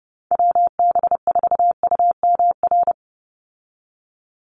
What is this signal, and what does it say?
Morse code, heard as a keyed tone, about 30 wpm, 700 Hz. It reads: WB4UMR
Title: USA Call Signs 30wpm